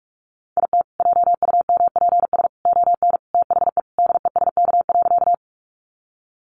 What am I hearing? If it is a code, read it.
Morse code, heard as a keyed tone, about 44 wpm, 700 Hz. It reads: IT JUMPS ON THE DESK.
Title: Additional Easy Phrases 45wpm